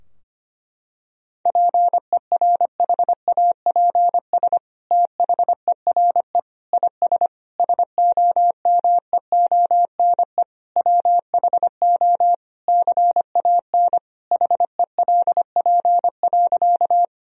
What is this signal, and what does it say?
Morse code, heard as a keyed tone, about 25 wpm, 700 Hz. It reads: PERHAPS THERE IS SOMEONE WHO CAN HELP.
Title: Sentences from Top 1000 Words - Review Part 4 - 25wpm